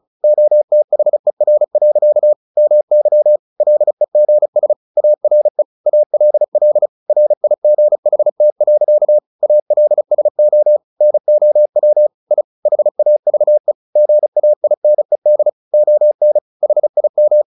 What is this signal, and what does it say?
Morse code, heard as a keyed tone, about 34 wpm, 600 Hz. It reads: OTHER. MY LEGS ARE ALL RIGHT. ALSO NOW I HAVE GAINED ON HIM
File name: the old man and the sea 2 35wpm